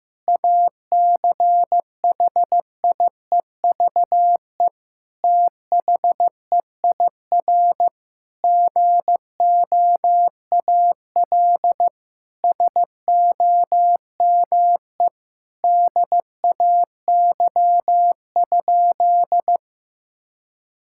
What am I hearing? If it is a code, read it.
Morse code, heard as a keyed tone, about 15 wpm, 700 Hz. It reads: ACHIEVE THEIR GOAL SOME DAY?
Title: Sentences from Top 1000 Words - Review Part 6 - 15wpm